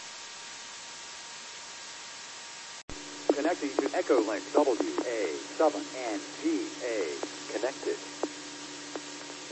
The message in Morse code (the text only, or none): none